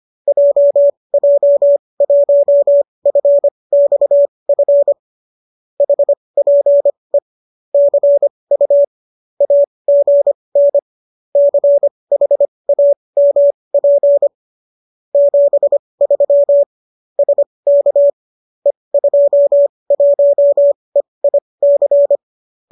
Morse code, heard as a keyed tone, about 25 wpm, spaced as for 20 wpm, 570 Hz. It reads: JJ1FXF HPE CU AGN CHAMP 73 SK E21EIC
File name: qso208 20